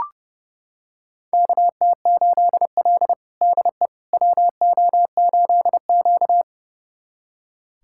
Morse code, heard as a keyed tone, about 32 wpm, 700 Hz. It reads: KT8L DE WO8Q
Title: USA Callsign DE Callsign - 4 Characters - 30wpm